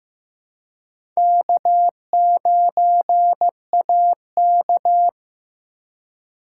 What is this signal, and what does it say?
Morse code, heard as a keyed tone, about 15 wpm, 700 Hz. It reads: K9AK
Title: USA Call Signs - 4 Characters - 2x Character Spacing - 15wpm